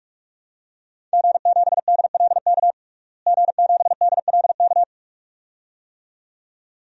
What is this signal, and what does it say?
Morse code, heard as a keyed tone, about 46 wpm, 700 Hz. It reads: G7DLK G7DLK
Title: DXCC-100 Call Signs - 5 Characters - Repeated 2x - 45wpm